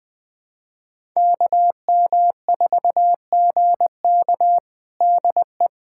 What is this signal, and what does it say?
Morse code, heard as a keyed tone, about 20 wpm, 700 Hz. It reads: KM4GK DE
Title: USA Callsign DE Callsign 20wpm